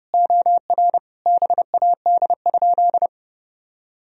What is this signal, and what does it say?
Morse code, heard as a keyed tone, about 32 wpm, 700 Hz. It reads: OR BAD?
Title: Sentences from Top 1000 Words - Review Part 1 - 30wpm